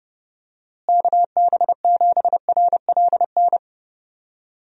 Morse code, heard as a keyed tone, about 30 wpm, 700 Hz. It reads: KB7RLN
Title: USA Call Signs - 6 Characters - Repeated 2x - 30wpm